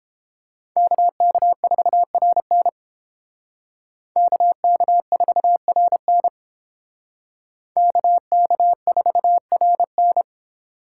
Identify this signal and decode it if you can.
Morse code, heard as a keyed tone, about 29 wpm, 700 Hz. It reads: KK4RN KK4RN KK4RN